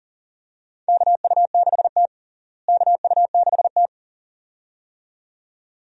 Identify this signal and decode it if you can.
Morse code, heard as a keyed tone, about 40 wpm, 700 Hz. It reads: KU6T KU6T
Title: USA Call Signs - 4 Characters - Repeated 2x - 40wpm